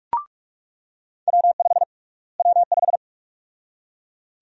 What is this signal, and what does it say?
Morse code, heard as a keyed tone, about 44 wpm, 700 Hz. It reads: W5 W5
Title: USA Call Sign Prefixes - Repeated 2x - 45wpm